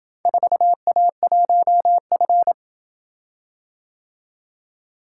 Morse code, heard as a keyed tone, about 28 wpm, 700 Hz. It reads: 4A1F